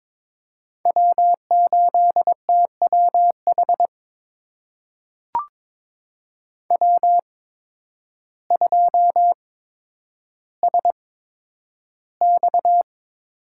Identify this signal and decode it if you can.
Morse code, heard as a keyed tone, about 22 wpm, spaced as for 10 wpm, 700 Hz. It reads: W8TWH W2SX
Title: USA Call Signs - 8x Character Spacing - 22wpm